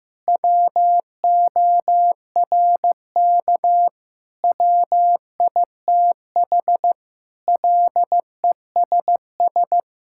Morse code, heard as a keyed tone, about 16 wpm, 700 Hz. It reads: WORK WITH LESS
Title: Sentences from Top 1000 Words - Review Part 5 - 15wpm